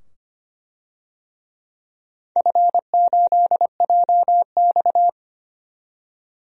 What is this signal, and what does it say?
Morse code, heard as a keyed tone, about 25 wpm, 700 Hz. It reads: F8JX
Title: DXCC-100 Call Signs - 4 Characters - 25wpm